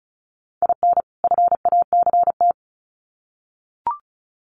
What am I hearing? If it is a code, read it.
Morse code, heard as a keyed tone, about 34 wpm, 700 Hz. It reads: IN FACT
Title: Sets of 2 Words 35wpm